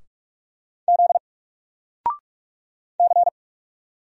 Morse code, heard as a keyed tone, about 44 wpm, 700 Hz. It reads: Z C
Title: Single Letter-Number Alt 45wpm